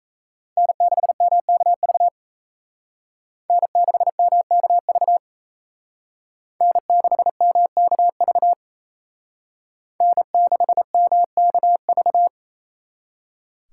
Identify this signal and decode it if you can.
Morse code, heard as a keyed tone, about 35 wpm, 700 Hz. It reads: N6MKV N6MKV N6MKV N6MKV